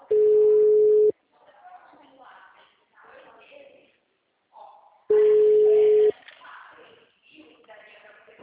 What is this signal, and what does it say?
Ringback tone (Europe, 425 Hz), with speech.